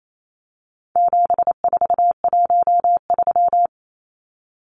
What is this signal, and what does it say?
Morse code, heard as a keyed tone, about 28 wpm, 700 Hz. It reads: 7413
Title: Numbers - 4-Digit 28wpm